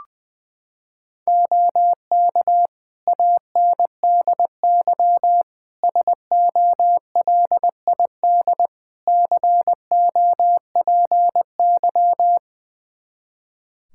Morse code, heard as a keyed tone, about 21 wpm, 700 Hz. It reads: OK ANDY SOLID COPY
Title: K1USN SST QSO Practice 20wpm